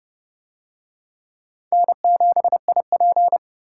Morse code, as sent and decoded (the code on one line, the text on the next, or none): -. --... .. .--.
N7IP